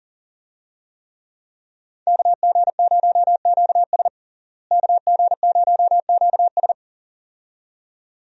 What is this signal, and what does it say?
Morse code, heard as a keyed tone, about 37 wpm, 700 Hz. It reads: KG0QS KG0QS
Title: USA Call Signs - 5 Characters - Repeated 2x - 40wpm